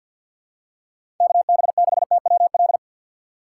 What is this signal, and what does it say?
Morse code, heard as a keyed tone, about 48 wpm, 700 Hz. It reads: KB6TWL
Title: USA Call Signs - 6 Characters - 4x Character Spacing - 50wpm